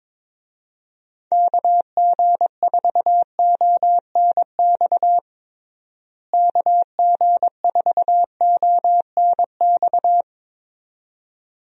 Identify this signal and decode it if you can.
Morse code, heard as a keyed tone, about 22 wpm, 700 Hz. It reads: KG4ONX KG4ONX
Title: USA Call Signs - 6 Characters - Repeated 2x - 22wpm